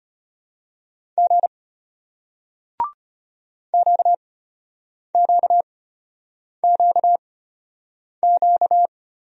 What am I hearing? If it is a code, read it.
Morse code, heard as a keyed tone, about 33 wpm, spaced as for 8 wpm, 700 Hz. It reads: G QQQQ